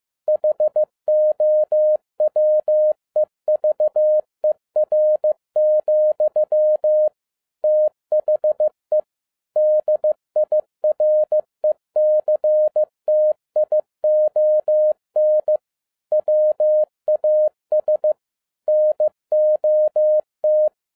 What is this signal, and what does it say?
Morse code, heard as a keyed tone, about 15 wpm, 610 Hz. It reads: HOWEVER, THE DIRECTION WAS NOT
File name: 2011-07-mp3 / n57 BACK FROM A BIG ACCIDENT 15wpm